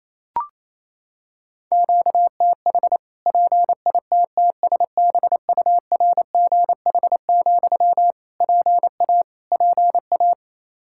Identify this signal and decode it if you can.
Morse code, heard as a keyed tone, about 28 wpm, 700 Hz. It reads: QTH PITTSBURGH, PA PA
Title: Practice QSO - Exchange 1 - Beginning - 28wpm